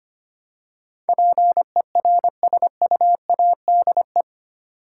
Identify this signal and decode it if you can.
Morse code, heard as a keyed tone, about 25 wpm, 700 Hz. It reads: PERSUADE
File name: Eight Letter Words - 25wpm